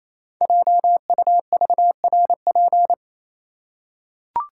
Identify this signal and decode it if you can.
Morse code, heard as a keyed tone, about 27 wpm, 700 Hz. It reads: JUVRP
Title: Code Groups - 5 Letters - 28wpm